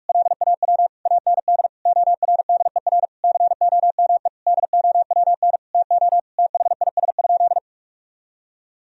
Morse code, heard as a keyed tone, about 44 wpm, 700 Hz. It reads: LAW AND ORDER COME DOWN TO THIS?
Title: Sentences from Top 1000 Words - Review Part 3 - 45wpm